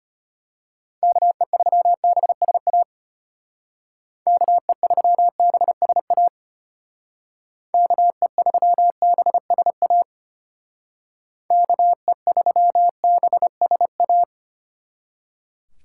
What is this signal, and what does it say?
Morse code, heard as a keyed tone, about 30 wpm, 700 Hz. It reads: KE3BSA KE3BSA KE3BSA KE3BSA
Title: Speed Racing - USA 2x3 Call Signs 25wpm